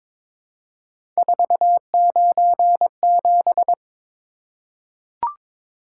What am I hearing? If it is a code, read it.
Morse code, heard as a keyed tone, about 22 wpm, 700 Hz. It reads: 497
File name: practice / 22wpm / Three Letter-Number Combinations - 22wpm